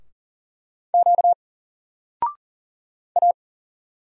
Morse code, heard as a keyed tone, about 40 wpm, 700 Hz. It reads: Q A
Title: Single Letter-Number Alt 40wpm